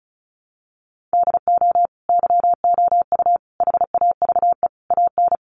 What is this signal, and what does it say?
Morse code, heard as a keyed tone, about 34 wpm, 700 Hz. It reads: DO YOU HAVE AN
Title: Sentences from Top 1300 Words 35wpm